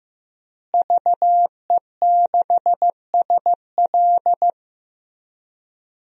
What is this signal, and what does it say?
Morse code, heard as a keyed tone, about 15 wpm, 700 Hz. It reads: VE6SL
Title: DXCC-100 Call Signs - 5 Characters - Repeated 2x - 15wpm